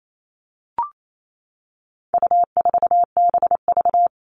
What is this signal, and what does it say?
Morse code, heard as a keyed tone, about 28 wpm, 700 Hz. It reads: U4BV
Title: All Cut Numbers - 4-Digit 28wpm